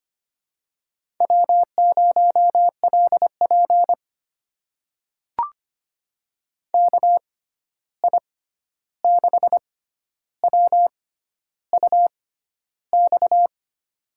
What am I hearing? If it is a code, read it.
Morse code, heard as a keyed tone, about 25 wpm, spaced as for 11 wpm, 700 Hz. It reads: W0LP KI6WUX
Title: USA Call Signs - 6x Character Spacing - 25wpm